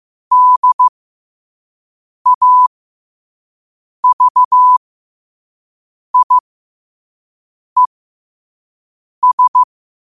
Morse code, heard as a keyed tone, about 15 wpm, spaced as for 5 wpm, 1000 Hz. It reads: DAVIES
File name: qotdfiles / 5wpm